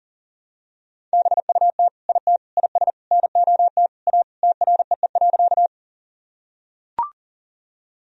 Morse code, heard as a keyed tone, about 37 wpm, 700 Hz. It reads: BUT IT IS NOT A TREE.
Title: Sentences from Top 1000 Words - Review Part 1 - 40wpm